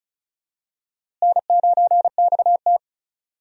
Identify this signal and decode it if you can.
Morse code, heard as a keyed tone, about 36 wpm, 700 Hz. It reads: N9XT
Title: USA Call Signs - 4 Characters - Repeated 3x - 35wpm